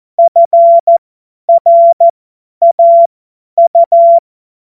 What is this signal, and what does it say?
Morse code, heard as a keyed tone, about 14 wpm, spaced as for 10 wpm, 680 Hz. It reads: FRAU
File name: Morsekurs / Witze / Witz 01. Februar 2022 50 F70